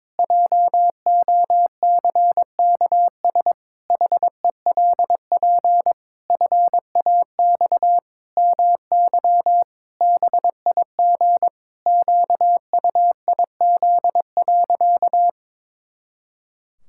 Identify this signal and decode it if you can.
Morse code, heard as a keyed tone, about 22 wpm, 700 Hz. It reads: JOCKS HELP FAX MY BIG QUIZ.
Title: Pangram Challenge v2 - 22wpm